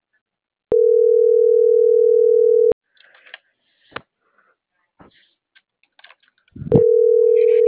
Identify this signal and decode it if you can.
Ringback tone (North America), 440 + 480 Hz.